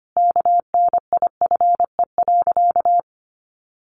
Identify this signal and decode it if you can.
Morse code, heard as a keyed tone, about 25 wpm, 700 Hz. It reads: KNIFE.
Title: A Sound Of Thunder 25wpm